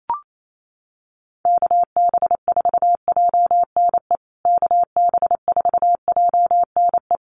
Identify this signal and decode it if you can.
Morse code, heard as a keyed tone, about 28 wpm, 700 Hz. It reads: KB4JNE KB4JNE
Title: USA Call Signs Repeated 3x 28wpm